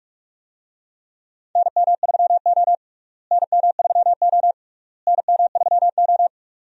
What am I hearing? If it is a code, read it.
Morse code, heard as a keyed tone, about 44 wpm, 700 Hz. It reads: NM3O NM3O NM3O
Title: USA Call Signs - 4 Characters - Repeated 3x - 45wpm